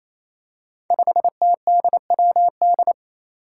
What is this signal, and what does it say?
Morse code, heard as a keyed tone, about 28 wpm, 700 Hz. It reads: 5TDWD
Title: Five Challenging Letter-Number Combinations 28wpm